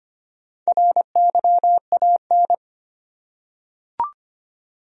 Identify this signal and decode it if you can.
Morse code, heard as a keyed tone, about 25 wpm, 700 Hz. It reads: RYAN
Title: USA Common Names - Max 5 Letters - 25wpm